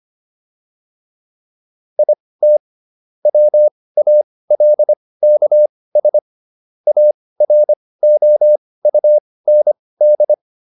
Morse code, heard as a keyed tone, about 25 wpm, spaced as for 18 wpm, 600 Hz. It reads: IT WALKS AROUND